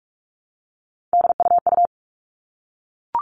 Morse code, heard as a keyed tone, about 44 wpm, 700 Hz. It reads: DUU
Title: All Cut Numbers - 3-Digit 45wpm